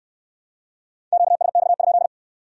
Morse code, heard as a keyed tone, about 64 wpm, 700 Hz. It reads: QSX?